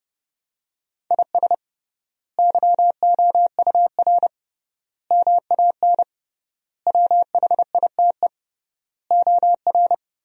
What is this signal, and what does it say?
Morse code, heard as a keyed tone, about 30 wpm, 700 Hz. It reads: IS YOUR MAN WHITE OR
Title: 3x Word-Spacing - Sentences from Top 400 Words - 30wpm